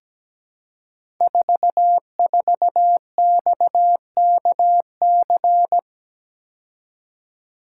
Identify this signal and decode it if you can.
Morse code, heard as a keyed tone, about 17 wpm, 700 Hz. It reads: 44XKC
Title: Code Groups - 5 Letter:Numbers - 17wpm